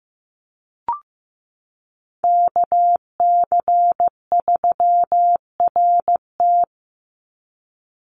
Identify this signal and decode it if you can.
Morse code, heard as a keyed tone, about 15 wpm, 700 Hz. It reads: KC3RT
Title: NA SOTA Chaser Callsigns 15wpm